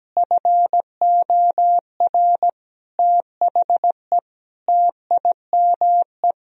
Morse code, heard as a keyed tone, about 17 wpm, 700 Hz. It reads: FOR THE TIME
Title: Sentences from Top 1000 Words - Review Part 4 - 17wpm